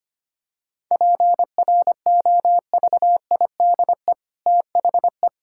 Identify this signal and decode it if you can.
Morse code, heard as a keyed tone, about 25 wpm, 700 Hz. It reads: PROVIDE THE
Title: Sentences from Top 1000 Words - Review Part 5 - 25wpm